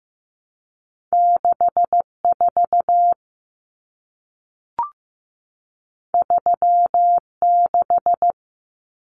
Morse code, heard as a keyed tone, about 15 wpm, 700 Hz. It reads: 64 36
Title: Common Cut Numbers - 2-Digit 15wpm